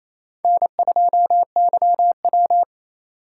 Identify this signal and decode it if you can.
Morse code, heard as a keyed tone, about 27 wpm, 700 Hz. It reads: N2YW
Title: USA Call Signs - 4 Characters - 2x Character Spacing - 28wpm